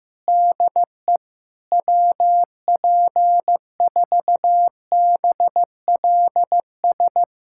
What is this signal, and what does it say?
Morse code, heard as a keyed tone, about 15 wpm, 700 Hz. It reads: DE WP4BLS
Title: USA Callsign DE Callsign - 6 Characters - 15wpm